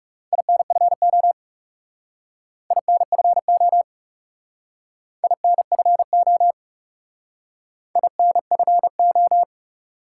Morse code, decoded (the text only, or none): INFO INFO INFO INFO